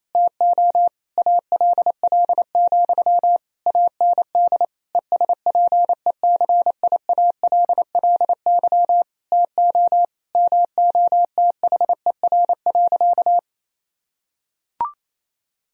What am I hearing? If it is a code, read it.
Morse code, heard as a keyed tone, about 28 wpm, 700 Hz. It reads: TO ALL, AND ESPECIALLY TO MOTHER.
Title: Sentences from Top 1000 Words - Review Part 4 - 28wpm